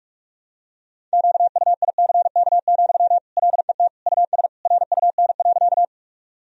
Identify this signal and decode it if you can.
Morse code, heard as a keyed tone, about 42 wpm, 700 Hz. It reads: QUICK, LET US RUN.
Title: Sentences from Top 1000 Words - Review Part 1 - 45wpm